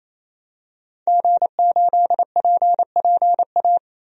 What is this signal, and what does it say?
Morse code, heard as a keyed tone, about 28 wpm, 700 Hz. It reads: G8PPA
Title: DXCC-100 Call Signs - 5 Characters - 28wpm